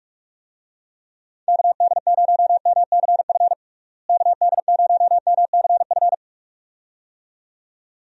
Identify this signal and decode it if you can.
Morse code, heard as a keyed tone, about 41 wpm, 700 Hz. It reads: KD0MCF KD0MCF
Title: USA Call Signs - 6 Characters - Repeated 2x - 45wpm